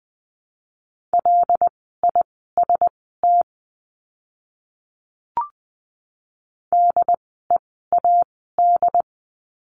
Morse code, heard as a keyed tone, about 20 wpm, spaced as for 15 wpm, 700 Hz. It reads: LIST DEAD
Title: 075 Lesson 13f - Letter L - Words 20wpm